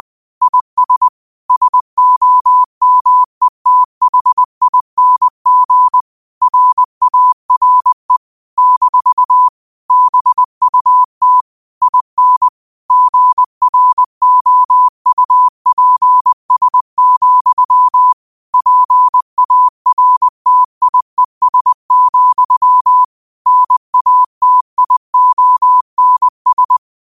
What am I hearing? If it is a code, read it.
Morse code, heard as a keyed tone, about 20 wpm, 1000 Hz. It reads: IS SOMETHING RARE - BUT IN GROUPS, PARTIES, NATIONS
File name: qotdfiles / 20wpm